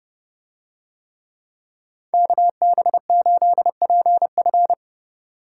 Morse code, heard as a keyed tone, about 27 wpm, 700 Hz. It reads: KB8PF